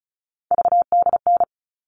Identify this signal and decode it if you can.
Morse code, heard as a keyed tone, about 34 wpm, 700 Hz. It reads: VDN
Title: All Cut Numbers - 3-Digit 35wpm